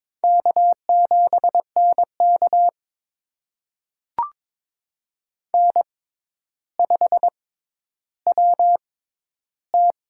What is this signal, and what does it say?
Morse code, heard as a keyed tone, about 22 wpm, spaced as for 11 wpm, 700 Hz. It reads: K7NK N5WT
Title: USA Call Signs - 4 Characters - 6x Character Spacing - 22wpm